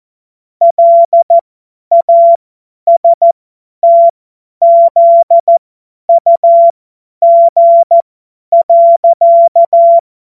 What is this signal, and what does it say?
Morse code, heard as a keyed tone, about 14 wpm, spaced as for 10 wpm, 680 Hz. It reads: LASTZUG.
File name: Morsekurs / Witze / Witz 01. März 2023 50 F70